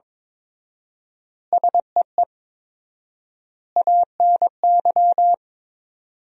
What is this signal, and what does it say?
Morse code, heard as a keyed tone, about 22 wpm, 700 Hz. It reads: SEE ANY
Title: 4x Word-Spacing - Sentences from Top 500 Words - 22wpm